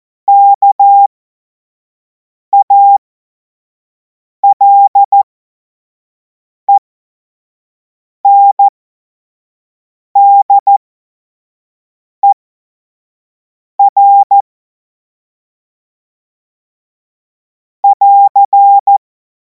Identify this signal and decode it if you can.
Morse code, heard as a keyed tone, about 14 wpm, spaced as for 5 wpm, 800 Hz. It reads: KALENDER <AR>